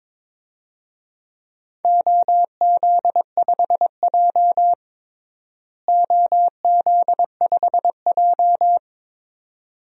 Morse code, heard as a keyed tone, about 22 wpm, 700 Hz. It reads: OZ5J OZ5J
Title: DXCC-100 Call Signs - 4 Characters - Repeated 2x - 22wpm